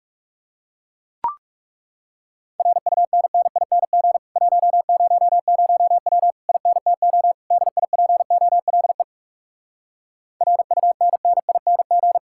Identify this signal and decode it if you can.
Morse code, heard as a keyed tone, about 40 wpm, 700 Hz. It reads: RUNNING 100W INTO DIPOLE RUNNING